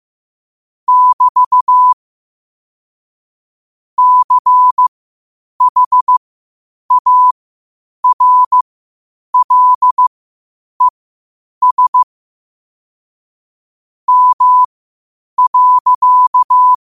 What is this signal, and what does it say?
Morse code, heard as a keyed tone, about 15 wpm, spaced as for 8 wpm, 1000 Hz. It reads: <BT> CHARLES M.
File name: qotdfiles / 8wpm